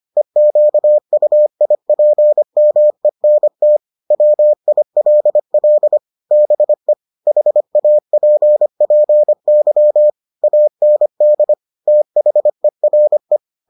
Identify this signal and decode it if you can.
Morse code, heard as a keyed tone, about 25 wpm, 600 Hz. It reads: EQUIPMENT WILL BE HAPPY AND THERE